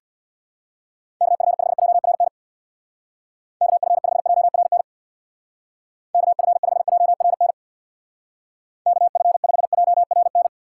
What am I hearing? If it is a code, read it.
Morse code, heard as a keyed tone, about 62 wpm, 700 Hz. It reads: KF5JRN KF5JRN KF5JRN KF5JRN
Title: Speed Racing - USA 2x3 Call Signs 50wpm